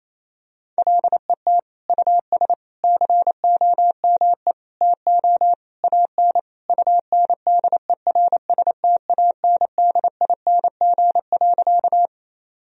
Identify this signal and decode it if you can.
Morse code, heard as a keyed tone, about 28 wpm, 700 Hz. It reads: LET US COME TO AN UNDERSTANDING.
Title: Sentences from Top 1000 Words - Review Part 3 - 28wpm